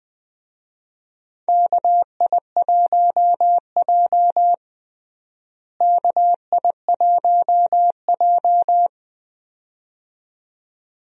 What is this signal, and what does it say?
Morse code, heard as a keyed tone, about 21 wpm, 700 Hz. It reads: KI1J KI1J
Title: USA Call Signs - 4 Characters - Repeated 2x - 20wpm